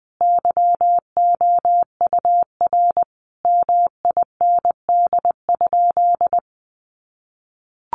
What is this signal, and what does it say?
Morse code, heard as a keyed tone, about 20 wpm, 700 Hz. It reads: YOUR MIND?
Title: Sentences from Top 1400 Words 20wpm